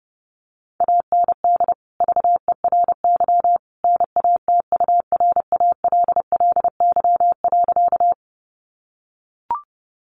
Morse code, heard as a keyed tone, about 30 wpm, 700 Hz. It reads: AND VERY NATURALLY.
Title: Sentences from Top 1300 Words 30wpm